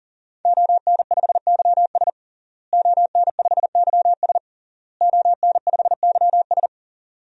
Morse code, heard as a keyed tone, about 40 wpm, 700 Hz. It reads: ON5YS ON5YS ON5YS
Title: DXCC-100 Call Signs - 5 Characters - Repeated 3x - 40wpm